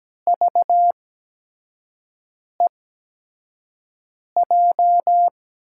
Morse code, heard as a keyed tone, about 17 wpm, spaced as for 5 wpm, 700 Hz. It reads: VEJ
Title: USA Call Signs - 6 Characters - 8x Character Spacing - 17wpm